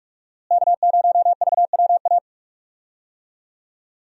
Morse code, heard as a keyed tone, about 44 wpm, 700 Hz. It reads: K0VWA
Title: USA Call Signs - 5 Characters - Repeated 3x - 45wpm